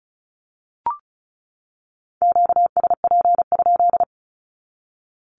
Morse code, heard as a keyed tone, about 34 wpm, 700 Hz. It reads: QSP?
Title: Q-Codes 35wpm